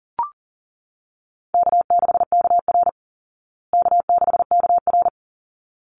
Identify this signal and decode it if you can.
Morse code, heard as a keyed tone, about 40 wpm, 700 Hz. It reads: K6KR K6KR
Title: Easy Call Signs Repeated 2x - 40wpm